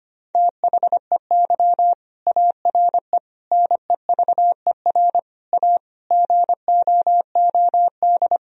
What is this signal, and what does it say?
Morse code, heard as a keyed tone, about 25 wpm, 700 Hz. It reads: THEY ARE NEVER A GOOD
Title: Sentences from Top 1000 Words - Review Part 4 - 25wpm